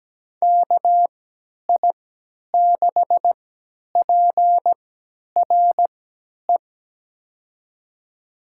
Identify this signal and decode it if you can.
Morse code, heard as a keyed tone, about 17 wpm, spaced as for 10 wpm, 700 Hz. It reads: KI6PRE
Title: USA Call Signs - 6 Characters - 3x Character Spacing - 17wpm